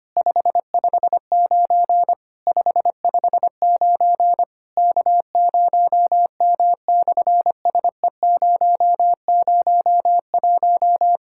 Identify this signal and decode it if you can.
Morse code, heard as a keyed tone, about 24 wpm, 700 Hz. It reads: 559 559 K0M/SE001
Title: SOTA QSO Practice 25wpm